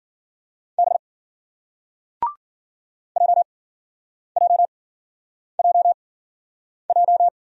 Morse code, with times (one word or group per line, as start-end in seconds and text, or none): 0.8-1.0 B
3.2-7.3 JJJJ